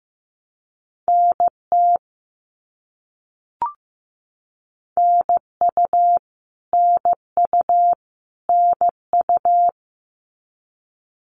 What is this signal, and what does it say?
Morse code, heard as a keyed tone, about 15 wpm, 700 Hz. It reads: NT NU NU NU
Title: Canadian Provinces and Territories - Repeated 3x - 15wpm